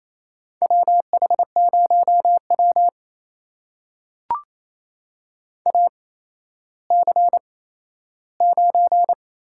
Morse code, heard as a keyed tone, about 28 wpm, spaced as for 14 wpm, 700 Hz. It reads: WH0W AC9
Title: USA Call Signs - 4 Characters - 8x Character Spacing - 28wpm